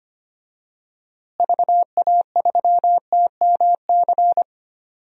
Morse code, heard as a keyed tone, about 25 wpm, 700 Hz. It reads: VA3TMC
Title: DXCC-100 Call Signs - 6 Characters - Repeated 2x - 25wpm